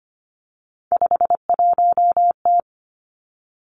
Morse code, heard as a keyed tone, about 25 wpm, 700 Hz. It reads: 51T
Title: Three Challenging Letter-Number Combinations 25wpm